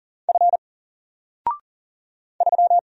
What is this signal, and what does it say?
Morse code, heard as a keyed tone, about 40 wpm, 700 Hz. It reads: F 3
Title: Single Character (Letter:Number:Punctuation:Prosign) Alt 40wpm